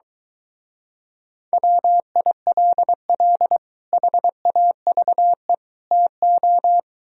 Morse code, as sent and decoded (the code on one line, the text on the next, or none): .-- .. .-.. .-.. / .... .- ...- . / - ---
WILL HAVE TO